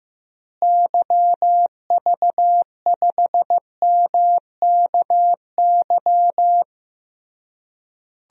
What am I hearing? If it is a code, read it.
Morse code, heard as a keyed tone, about 15 wpm, 700 Hz. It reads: YV5MKY
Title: DXCC-100 Call Signs - 6 Characters - Repeated 3x - 15wpm